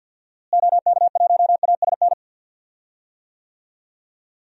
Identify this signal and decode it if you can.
Morse code, heard as a keyed tone, about 45 wpm, 700 Hz. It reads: OK1ASN
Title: DXCC-100 Call Signs - 6 Characters - Repeated 3x - 50wpm